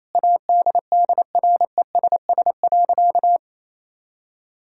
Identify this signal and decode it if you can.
Morse code, heard as a keyed tone, about 28 wpm, 700 Hz. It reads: ADDRESS.
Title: Sentences from Top 1000 Words - Review Part 6 - 28wpm